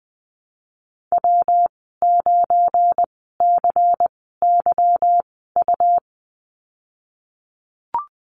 Morse code, heard as a keyed tone, about 20 wpm, spaced as for 15 wpm, 700 Hz. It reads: W9CYU
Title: 193 Lesson 31g - Letter v - Call Signs 20wpm